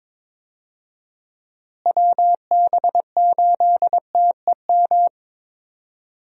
Morse code, heard as a keyed tone, about 22 wpm, 700 Hz. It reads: WB8TEM
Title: USA Call Signs - 6 Characters - Repeated 3x - 22wpm